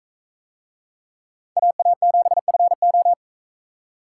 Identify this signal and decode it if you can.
Morse code, heard as a keyed tone, about 40 wpm, 700 Hz. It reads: AA7FO